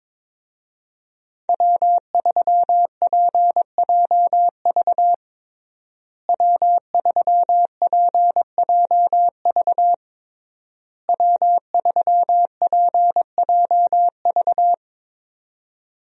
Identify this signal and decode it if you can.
Morse code, heard as a keyed tone, about 22 wpm, 700 Hz. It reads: W3PJV W3PJV W3PJV
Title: USA Call Signs - 5 Characters - Repeated 3x - 22wpm